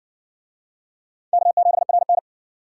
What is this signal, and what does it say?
Morse code, heard as a keyed tone, about 55 wpm, 700 Hz. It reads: K7RN